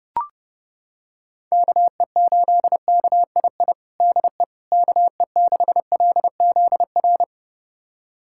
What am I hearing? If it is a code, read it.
Morse code, heard as a keyed tone, about 30 wpm, 700 Hz. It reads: KE8KII DE KE6LZR
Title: USA Callsign DE Callsign - 6 Characters - 30wpm